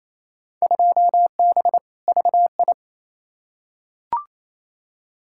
Morse code, heard as a keyed tone, about 28 wpm, 700 Hz. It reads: 2B VI
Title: Field Day Exchanges Encore - 28wpm